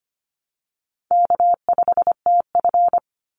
Morse code, heard as a keyed tone, about 25 wpm, 700 Hz. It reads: K5TF
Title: Call Signs Repeated 2x - 25wpm